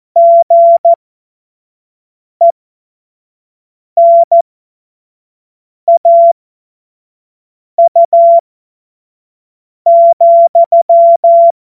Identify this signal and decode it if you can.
Morse code, heard as a keyed tone, about 14 wpm, spaced as for 5 wpm, 680 Hz. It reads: GENAU,
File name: Morsekurs / Witze / Witz 03. Mai 2022 25 F70